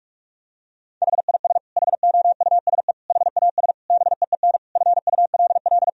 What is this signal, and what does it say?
Morse code, heard as a keyed tone, about 48 wpm, 700 Hz. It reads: HIS HOUSE HAS BEEN FULL
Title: Sentences from Top 1000 Words - Review Part 3 - 45wpm